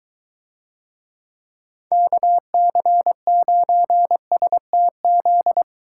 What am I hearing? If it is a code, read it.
Morse code, heard as a keyed tone, about 23 wpm, 700 Hz. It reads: KC9STZ